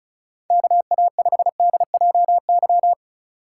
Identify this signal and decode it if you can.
Morse code, heard as a keyed tone, about 34 wpm, 700 Hz. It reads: KA5DJY
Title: USA Call Signs - 6 Characters - 3x Character Spacing - 35wpm